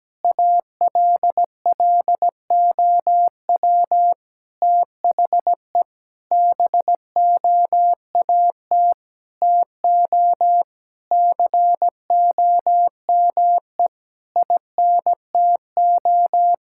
Morse code, heard as a keyed tone, about 17 wpm, 700 Hz. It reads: ALLOW THE BOAT TO COME INTO
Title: Sentences from Top 1000 Words - Review Part 5 - 17wpm